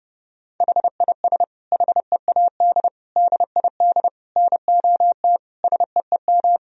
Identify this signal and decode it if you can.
Morse code, heard as a keyed tone, about 30 wpm, 700 Hz. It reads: HIS HEAD DID NOT SEEM
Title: Sentences from Top 1000 Words - Review Part 2 - 30wpm